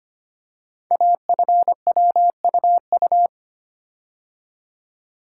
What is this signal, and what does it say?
Morse code, heard as a keyed tone, about 25 wpm, 700 Hz. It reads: AFWUU
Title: Code Groups - 5 Letters - 25wpm